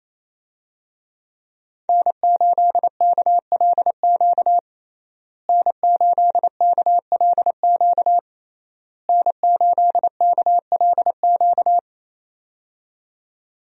Morse code, heard as a keyed tone, about 28 wpm, 700 Hz. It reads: N8KLQ N8KLQ N8KLQ
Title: USA Call Signs - 5 Characters - Repeated 3x - 28wpm